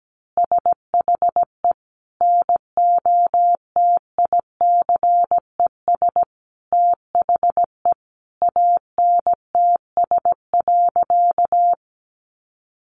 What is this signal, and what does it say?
Morse code, heard as a keyed tone, about 17 wpm, 700 Hz. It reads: SHE NOTICES THE ANTS.
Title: Additional Easy Phrases 17wpm